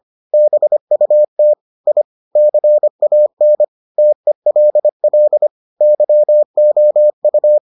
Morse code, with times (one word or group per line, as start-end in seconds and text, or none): 0.3-1.5 BUT
1.9-2.0 I
2.4-3.6 CAN
4.0-5.5 TELL
5.8-7.6 YOU